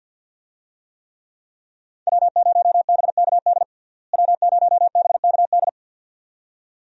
Morse code, heard as a keyed tone, about 52 wpm, 700 Hz. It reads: W0BKD W0BKD
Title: USA Call Signs - 5 Characters - Repeated 2x - 50wpm